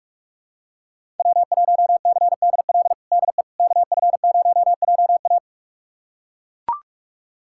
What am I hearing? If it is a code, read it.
Morse code, heard as a keyed tone, about 42 wpm, 700 Hz. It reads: W1CDL DE KF0JA
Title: USA Callsign DE Callsign - 5 Characters - 45wpm